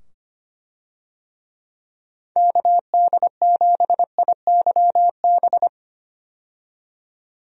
Morse code, heard as a keyed tone, about 25 wpm, 700 Hz. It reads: KD7IYB
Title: USA Call Signs - 6 Characters - 25wpm